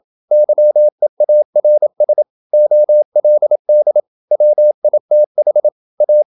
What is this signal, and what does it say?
Morse code, heard as a keyed tone, about 27 wpm, 600 Hz. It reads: YEARS OLD WITH A
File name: Never 0027wpm